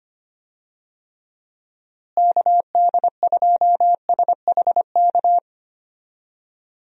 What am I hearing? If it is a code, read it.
Morse code, heard as a keyed tone, about 25 wpm, 700 Hz. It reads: KD2SHK
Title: USA Call Signs - 6 Characters - Repeated 3x - 25wpm